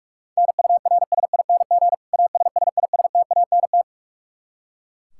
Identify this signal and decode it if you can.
Morse code, heard as a keyed tone, about 42 wpm, 700 Hz. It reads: NURSING ASSISTANT
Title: Practice QSO - Exchange 3 - More Info - 45wpm